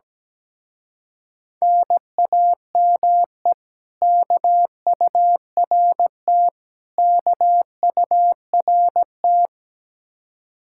Morse code, heard as a keyed tone, about 17 wpm, 700 Hz. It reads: NAME KURT KURT
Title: Speed-Racing Common QSO Phrases - 17wpm